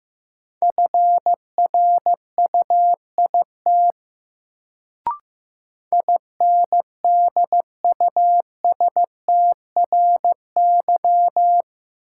Morse code, heard as a keyed tone, about 15 wpm, 700 Hz. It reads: FRUIT INDUSTRY
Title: Sentences from Top 1300 Words - New Word Drill Alt - 15wpm